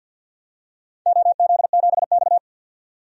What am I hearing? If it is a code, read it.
Morse code, heard as a keyed tone, about 51 wpm, 700 Hz. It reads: OZ7X
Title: DXCC-100 Call Signs - 4 Characters - 50wpm